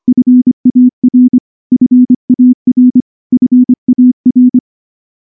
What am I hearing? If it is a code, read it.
Morse code, heard as a keyed tone, about 24 wpm, 250 Hz. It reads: FAR FAR FAR
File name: morse / morse-training-25WPM